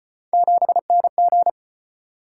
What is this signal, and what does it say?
Morse code, heard as a keyed tone, about 35 wpm, 700 Hz. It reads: 7NG